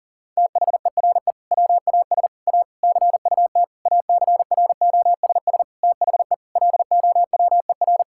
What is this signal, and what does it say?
Morse code, heard as a keyed tone, about 37 wpm, 700 Hz. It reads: THERE WAS A CUT ACROSS THE LOWER
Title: Sentences from Top 1000 Words - Review Part 5 - 40wpm